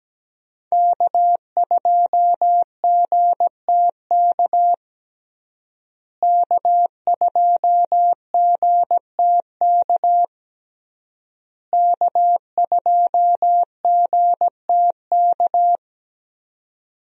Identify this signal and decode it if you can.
Morse code, heard as a keyed tone, about 17 wpm, 700 Hz. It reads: K2GTK K2GTK K2GTK
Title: USA Call Signs - 5 Characters - Repeated 3x - 17wpm